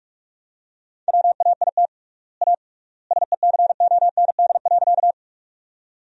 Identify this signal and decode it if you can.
Morse code, heard as a keyed tone, about 45 wpm, 700 Hz. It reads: WAIT A SECOND.
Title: 3x Word-Spacing - Sentences from Top 500 Words - 45wpm